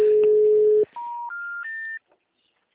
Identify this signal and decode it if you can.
Special information tone.